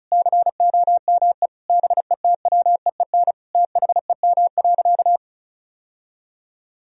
Morse code, heard as a keyed tone, about 36 wpm, 700 Hz. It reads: COME BETWEEN THEM.
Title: Sentences from Top 1000 Words - Review Part 1 - 35wpm